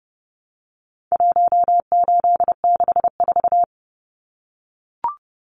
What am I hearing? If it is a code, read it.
Morse code, heard as a keyed tone, about 30 wpm, 700 Hz. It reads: 1864
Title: Numbers - 4-Digit 30wpm